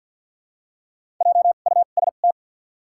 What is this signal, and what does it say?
Morse code, heard as a keyed tone, about 48 wpm, spaced as for 35 wpm, 700 Hz. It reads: JUST
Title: 2x Character Spacing - Top 300 Words - 50wpm